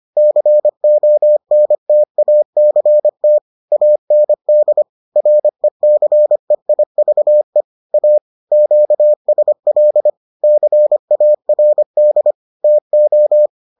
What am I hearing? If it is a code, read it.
Morse code, heard as a keyed tone, about 25 wpm, 600 Hz. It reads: CONTACT AND RECEIVE A QSL CARD TO